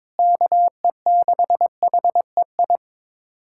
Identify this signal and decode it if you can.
Morse code, heard as a keyed tone, about 22 wpm, 700 Hz. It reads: KE6HEI
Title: USA Callsign DE Callsign 22wpm